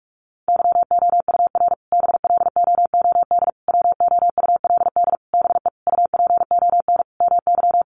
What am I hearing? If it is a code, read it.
Morse code, heard as a keyed tone, about 44 wpm, 700 Hz. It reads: YOUR BLOOD WOULD BE UPON MY
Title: Sentences from Top 1100 Words 45wpm